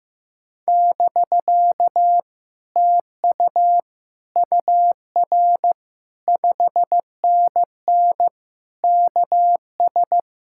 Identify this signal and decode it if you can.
Morse code, heard as a keyed tone, about 15 wpm, 700 Hz. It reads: <BK> TU UR 5NN KS
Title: Speed-Racing Common QSO Phrases Encore - 15wpm